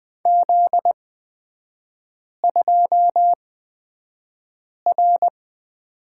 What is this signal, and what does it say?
Morse code, heard as a keyed tone, about 20 wpm, spaced as for 5 wpm, 700 Hz. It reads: Z2R
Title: 243 Lesson 39e - Letter z - Three Characters 20wpm